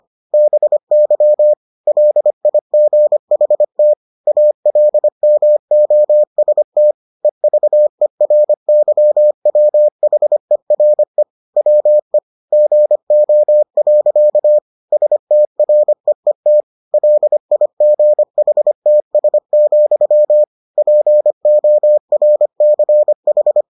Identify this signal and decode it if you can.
Morse code, heard as a keyed tone, about 25 wpm, 600 Hz. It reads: BY LIGHT ALMOST EVERYWHERE WE GO. STREET LIGHTS, PORCH